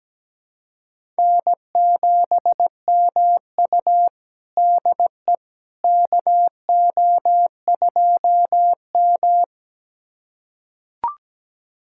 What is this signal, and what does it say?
Morse code, heard as a keyed tone, about 17 wpm, 700 Hz. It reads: N7MU DE KO2M
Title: USA Callsign DE Callsign - 4 Characters - 17wpm